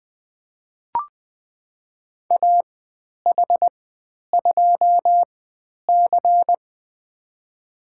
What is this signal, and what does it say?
Morse code, heard as a keyed tone, about 20 wpm, spaced as for 10 wpm, 700 Hz. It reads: AH2C
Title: 102 Lesson 17g - Letter u - Call Signs 20wpm